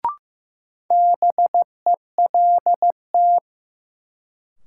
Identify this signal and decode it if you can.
Morse code, heard as a keyed tone, about 15 wpm, 700 Hz. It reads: BELT
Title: Sentences from Top 1400 Words - New Word Drill - 15wpm